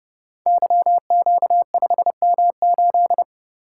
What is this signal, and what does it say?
Morse code, heard as a keyed tone, about 31 wpm, 700 Hz. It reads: YQ5M8
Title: Code Groups - 5 Letter:Numbers - 30wpm